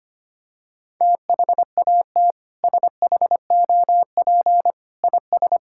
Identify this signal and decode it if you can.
Morse code, heard as a keyed tone, about 25 wpm, 700 Hz. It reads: THAT SHOP IS
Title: Sentences from Top 1600 Words 25wpm